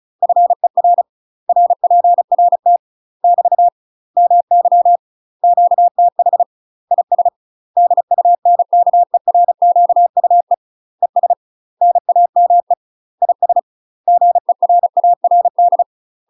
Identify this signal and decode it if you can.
Morse code, heard as a keyed tone, about 34 wpm, 700 Hz. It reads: FER RPRT <BT> MY QTH IS DUNKERQUE ES NAME IS GERARD
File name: UK MASTER QSOs TRAINING / 35WPM / UK MASTER QSOs 0039